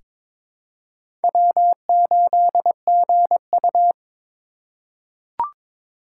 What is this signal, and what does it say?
Morse code, heard as a keyed tone, about 22 wpm, 700 Hz. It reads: W8GU
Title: K1USN SST QSO Practice - 22wpm